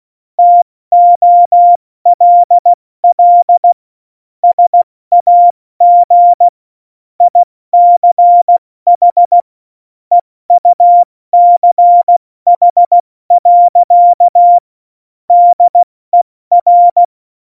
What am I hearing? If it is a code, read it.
Morse code, heard as a keyed tone, about 16 wpm, 700 Hz. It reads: TOLL SAG ICH EUCH. DER
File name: Morsekurs / Witze / Witz 29. Dezember 2020, 70 F80